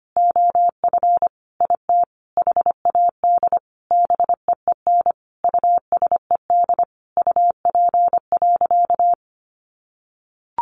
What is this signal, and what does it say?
Morse code, heard as a keyed tone, about 25 wpm, 700 Hz. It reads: OF IT HAD BEEN USED UP.
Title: Sentences from Top 900 Words 25wpm